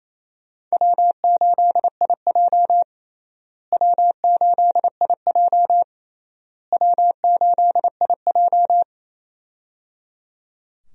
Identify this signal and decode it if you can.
Morse code, heard as a keyed tone, about 28 wpm, 700 Hz. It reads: W8IJ W8IJ W8IJ
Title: USA Call Signs - 4 Characters - Repeated 3x - 28wpm